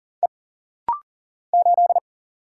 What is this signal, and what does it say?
Morse code, heard as a keyed tone, about 40 wpm, 700 Hz. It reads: E 8
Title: Single Letter-Number Alt - Rapid-Fire 40wpm